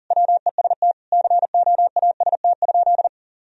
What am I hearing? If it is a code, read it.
Morse code, heard as a keyed tone, about 37 wpm, 700 Hz. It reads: WEST COAST?
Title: Sentences from Top 1000 Words - Review Part 4 - 40wpm